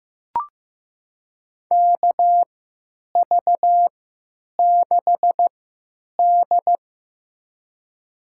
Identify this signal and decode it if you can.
Morse code, heard as a keyed tone, about 15 wpm, spaced as for 9 wpm, 700 Hz. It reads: KV6D
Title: USA Call Signs - 3x Character Spacing - 15wpm